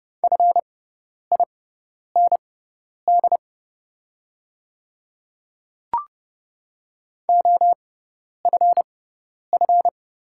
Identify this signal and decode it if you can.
Morse code, heard as a keyed tone, about 30 wpm, spaced as for 10 wpm, 700 Hz. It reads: FIND OFF
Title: 6x Character Spacing - Top 300 Words - 30wpm